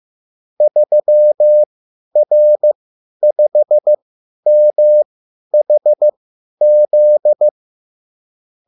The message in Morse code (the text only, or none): 3R5MHZ